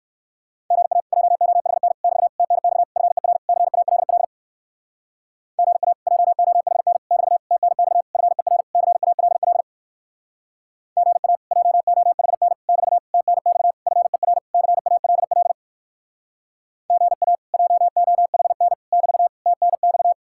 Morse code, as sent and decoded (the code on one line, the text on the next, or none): --. .- / .--- --- .... -. / -...- / - -. -..- / ..-. .-. / -.-. .- .-.. .-.. / --. .- / .--- --- .... -. / -...- / - -. -..- / ..-. .-. / -.-. .- .-.. .-.. / --. .- / .--- --- .... -. / -...- / - -. -..- / ..-. .-. / -.-. .- .-.. .-.. / --. .- / .--- --- .... -. / -...- / - -. -..-
GA JOHN <BT> TNX FER CALL GA JOHN <BT> TNX FER CALL GA JOHN <BT> TNX FER CALL GA JOHN <BT> TNX